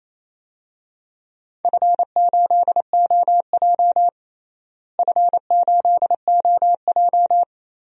Morse code, heard as a keyed tone, about 28 wpm, 700 Hz. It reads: F8OJ F8OJ
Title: DXCC-100 Call Signs - 4 Characters - Repeated 3x - 28wpm